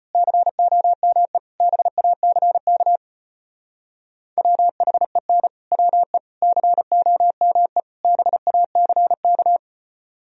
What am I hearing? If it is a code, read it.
Morse code, heard as a keyed tone, about 29 wpm, 700 Hz. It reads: COME BACK WHEN WE COME BACK
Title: Speed-Racing - Sets of 4 Words 25wpm